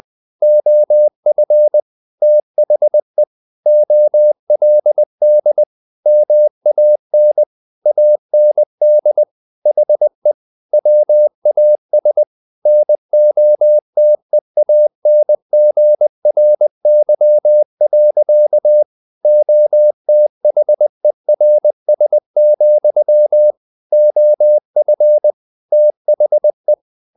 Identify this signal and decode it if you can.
Morse code, heard as a keyed tone, about 20 wpm, 600 Hz. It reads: OF THE OLD MAN AND HE WAS NOTEANGRY. OTHERS, OF THE
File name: the old man and the sea / the old man and the sea part 1 20wpm